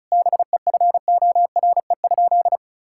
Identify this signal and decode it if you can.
Morse code, heard as a keyed tone, about 36 wpm, 700 Hz. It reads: BEFORE?
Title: Sentences from Top 1000 Words - Review Part 6 - 35wpm